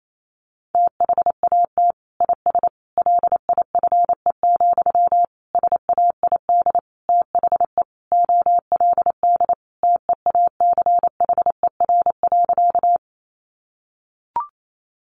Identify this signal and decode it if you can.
Morse code, heard as a keyed tone, about 28 wpm, 700 Hz. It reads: THAT IS LIFE, SAID THE OLD TEACHER.
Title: Sentences from Top 900 Words 28wpm